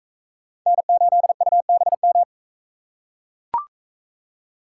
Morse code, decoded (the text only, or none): N8UBM